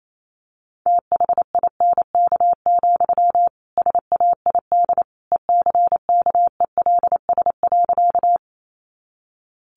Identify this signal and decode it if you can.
Morse code, heard as a keyed tone, about 28 wpm, 700 Hz. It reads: THINK, SAID ECKELS.
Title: A Sound Of Thunder 28wpm